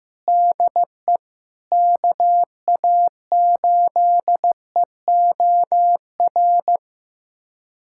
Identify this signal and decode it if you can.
Morse code, heard as a keyed tone, about 15 wpm, 700 Hz. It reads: DE KA8EOR
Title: USA Callsign DE Callsign - 6 Characters - 15wpm